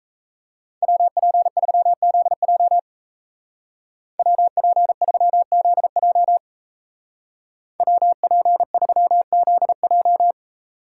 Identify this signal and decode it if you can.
Morse code, heard as a keyed tone, about 36 wpm, 700 Hz. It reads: WP3ZJ WP3ZJ WP3ZJ